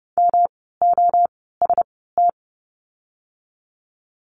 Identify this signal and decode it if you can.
Morse code, heard as a keyed tone, about 30 wpm, spaced as for 17 wpm, 700 Hz. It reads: MOST
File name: course / 30:10wpm / 115 Lesson 19f - Letter w - Words 30wpm